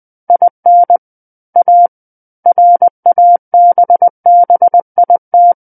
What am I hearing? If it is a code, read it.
Morse code, heard as a keyed tone, about 20 wpm, 700 Hz. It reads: IN A RABBIT
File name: 20 WPM